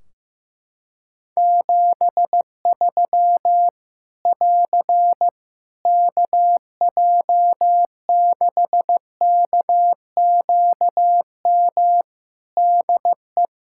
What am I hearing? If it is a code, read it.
Morse code, heard as a keyed tone, about 15 wpm, 700 Hz. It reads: 73 <AR> KJ6KQM DE
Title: Practice QSO - Exchange 4 - Ending - 15wpm